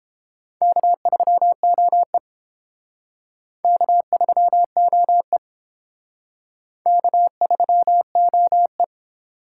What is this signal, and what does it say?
Morse code, heard as a keyed tone, about 30 wpm, 700 Hz. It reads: K3OE K3OE K3OE